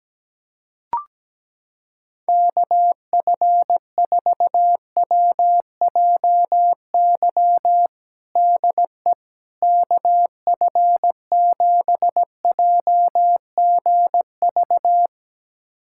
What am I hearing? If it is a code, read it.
Morse code, heard as a keyed tone, about 17 wpm, 700 Hz. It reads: KF4WJY DE KF7JGV
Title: USA Callsign DE Callsign 17wpm